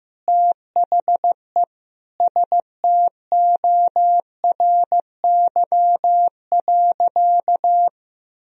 Morse code, heard as a keyed tone, about 15 wpm, 700 Hz. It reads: THE STORY.
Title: Sentences from Top 1000 Words - Review Part 2 - 15wpm